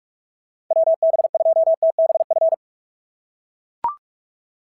Morse code, heard as a keyed tone, about 44 wpm, 690 Hz. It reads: WB2TBF
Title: SOTA QSO Practice 45wpm